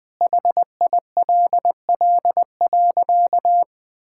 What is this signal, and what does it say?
Morse code, heard as a keyed tone, about 19 wpm, 700 Hz. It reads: HILL.
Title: Sentences with 4-Word Phrases - 20wpm